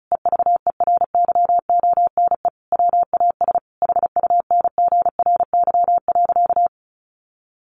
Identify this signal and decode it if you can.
Morse code, heard as a keyed tone, about 34 wpm, 700 Hz. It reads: EVERYONE WAS HUNGRY.
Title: Sentences from Top 1400 Words 35wpm